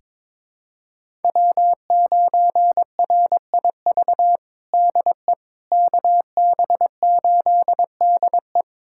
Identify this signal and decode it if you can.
Morse code, heard as a keyed tone, about 21 wpm, 700 Hz. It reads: W9RIV DE KB8DE
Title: USA Callsign DE Callsign - 5 Characters - 22wpm